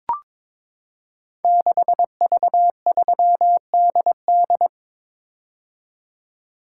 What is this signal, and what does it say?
Morse code, heard as a keyed tone, about 23 wpm, 700 Hz. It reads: 6V3DD
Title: Code Groups - 5 Letter:Numbers - 22wpm